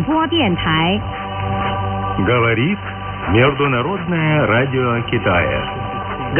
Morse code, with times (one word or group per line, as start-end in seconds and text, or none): none